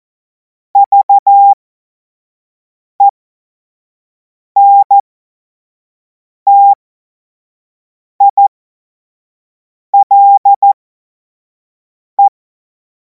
Morse code, heard as a keyed tone, about 14 wpm, spaced as for 5 wpm, 800 Hz. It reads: VENTILE
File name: Morsekurs / Witze / Witz 07. April 2026 25 F70